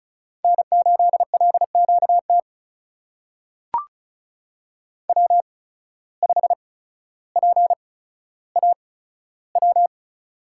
Morse code, heard as a keyed tone, about 34 wpm, spaced as for 15 wpm, 700 Hz. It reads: N8LQT W5PAW
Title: USA Call Signs - 8x Character Spacing - 35wpm